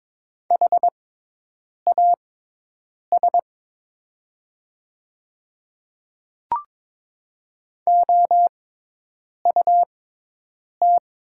Morse code, heard as a keyed tone, about 22 wpm, spaced as for 8 wpm, 700 Hz. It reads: HAS OUT
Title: 6x Character Spacing - Top 100 Words - 22wpm